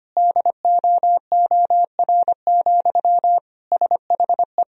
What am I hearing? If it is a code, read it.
Morse code, heard as a keyed tone, about 25 wpm, 700 Hz. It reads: DOOR, SHE
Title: Sentences from Top 1000 Words - Review Part 1 - 25wpm